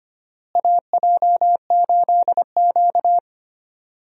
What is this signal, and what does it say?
Morse code, heard as a keyed tone, about 25 wpm, 700 Hz. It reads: AJ8Q
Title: USA Call Signs - 4 Characters - Repeated 3x - 25wpm